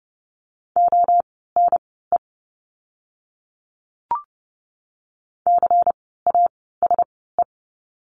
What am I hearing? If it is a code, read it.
Morse code, heard as a keyed tone, about 30 wpm, spaced as for 17 wpm, 700 Hz. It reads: ONE CASE
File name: Lesson 16f - Letter c - Words 30wpm